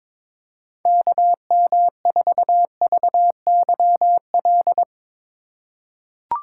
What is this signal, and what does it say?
Morse code, heard as a keyed tone, about 22 wpm, 700 Hz. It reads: KM4VYL
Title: USA Call Signs - 2x Character Spacing - 22wpm